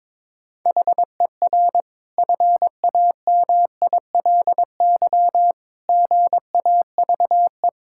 Morse code, heard as a keyed tone, about 22 wpm, 700 Hz. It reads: HER FAMILY GAVE
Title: Sentences from Top 1000 Words - Review Part 5 - 22wpm